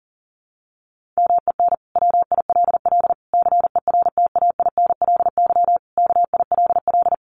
Morse code, heard as a keyed tone, about 40 wpm, 700 Hz. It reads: MEN WILL CERTAINLY KILL
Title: Sentences from Top 1500 Words 40wpm